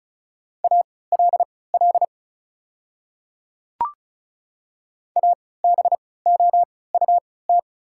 Morse code, heard as a keyed tone, about 36 wpm, spaced as for 20 wpm, 690 Hz. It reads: ALL ABOUT
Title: 3x Character Spacing - Top 100 Words - 35wpm